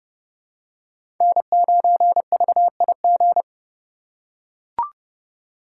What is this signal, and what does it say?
Morse code, heard as a keyed tone, about 30 wpm, 700 Hz. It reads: N9VIG
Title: USA Call Signs - 5 Characters - 4x Character Spacing - 30wpm